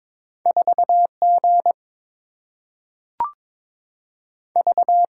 Morse code, heard as a keyed tone, about 22 wpm, 700 Hz. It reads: 4G V
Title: Single-Letter Single-Number 22wpm